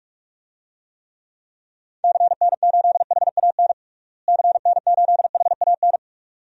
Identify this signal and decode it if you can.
Morse code, heard as a keyed tone, about 46 wpm, 700 Hz. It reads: CN8HAN CN8HAN
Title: DXCC-100 Call Signs - 6 Characters - Repeated 2x - 45wpm